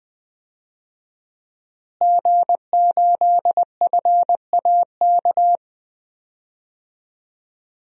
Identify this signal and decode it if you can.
Morse code, heard as a keyed tone, about 20 wpm, 700 Hz. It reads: G8FAK
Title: DXCC-100 Call Signs - 5 Characters - Repeated 2x - 20wpm